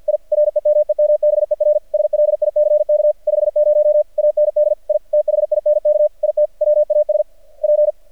Morse code, heard as a keyed tone, about 51 wpm, 600 Hz. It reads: I REMEMBER SLICK 50 AND I THINK IT WAS P
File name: archived / CRSnet